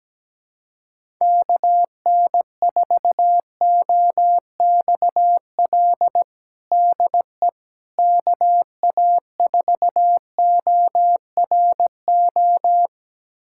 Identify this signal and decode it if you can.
Morse code, heard as a keyed tone, about 17 wpm, 700 Hz. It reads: KN4OXL DE KA4ORO
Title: USA Callsign DE Callsign 17wpm